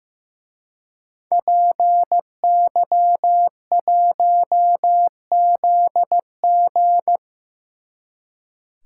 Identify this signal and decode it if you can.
Morse code, heard as a keyed tone, about 16 wpm, 700 Hz. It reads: PY1ZG
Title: Speed Racing - USA 2x2 Call Signs 15wpm